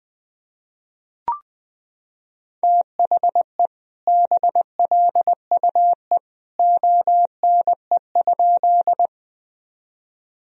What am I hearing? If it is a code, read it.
Morse code, heard as a keyed tone, about 20 wpm, 700 Hz. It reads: THE BLUE ONE?
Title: Sentences from Top 1000 Words - Review Part 3 - 20wpm